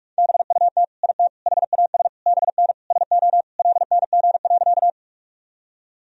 Morse code, heard as a keyed tone, about 46 wpm, 700 Hz. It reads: BUT IT HAS BN SO LNG.
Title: Sentences from Top 100 Words and Abbreviations 45wpm